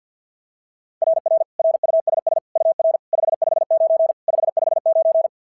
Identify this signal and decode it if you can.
Morse code, heard as a keyed tone, about 45 wpm, 660 Hz. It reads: RR RUSS UR 559 559
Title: SOTA QSO Practice 50wpm